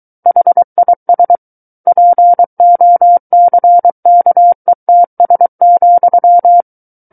Morse code, heard as a keyed tone, about 23 wpm, 700 Hz. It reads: HIS POCKETS,
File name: Book One / 14-Chapter 14 - 23 WPM